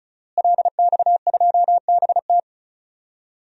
Morse code, heard as a keyed tone, about 34 wpm, 700 Hz. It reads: LX2BT
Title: DXCC-100 Call Signs - 5 Characters - Repeated 2x - 35wpm